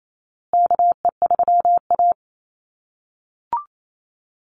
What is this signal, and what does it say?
Morse code, heard as a keyed tone, about 28 wpm, 700 Hz. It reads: KE3A
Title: USA Call Signs Repeated 2x 28wpm